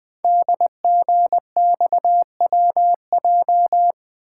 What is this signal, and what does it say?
Morse code, heard as a keyed tone, about 20 wpm, 700 Hz. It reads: DGXWJ
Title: Code Groups - 5 Letters - 20wpm